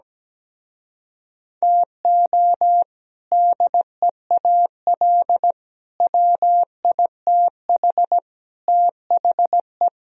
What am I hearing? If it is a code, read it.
Morse code, heard as a keyed tone, about 17 wpm, 700 Hz. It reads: TO DEAL WITH THE
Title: Speed-Racing - Sets of 4 Words 17wpm